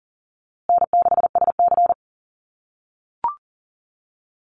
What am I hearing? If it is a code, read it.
Morse code, heard as a keyed tone, about 39 wpm, 700 Hz. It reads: N6SC
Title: USA Call Signs Repeated 3x 40wpm